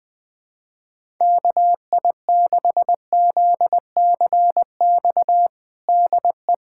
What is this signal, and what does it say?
Morse code, heard as a keyed tone, about 20 wpm, 700 Hz. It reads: KI6ZCX DE
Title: USA Callsign DE Callsign - 6 Characters - 20wpm